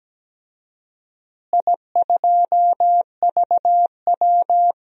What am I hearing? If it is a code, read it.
Morse code, heard as a keyed tone, about 17 wpm, 700 Hz. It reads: I2VW
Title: DXCC-100 Call Signs - 4 Characters - Repeated 3x - 17wpm